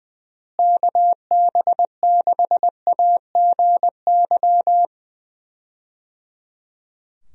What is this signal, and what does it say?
Morse code, heard as a keyed tone, about 21 wpm, 700 Hz. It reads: KB6AGY
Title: USA Call Signs - 6 Characters - Repeated 3x - 20wpm